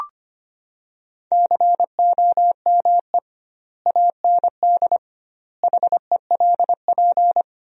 Morse code, heard as a keyed tone, about 25 wpm, 700 Hz. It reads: COME AND HELP
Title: 2x Word-Spacing - Sentences from Top 200 Words - 25wpm